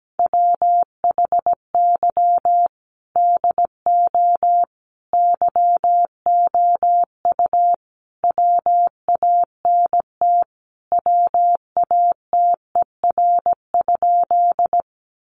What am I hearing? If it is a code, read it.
Morse code, heard as a keyed tone, about 17 wpm, 700 Hz. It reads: WHY DO YOU WANT WATER?
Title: Sentences from Top 200 Words 17wpm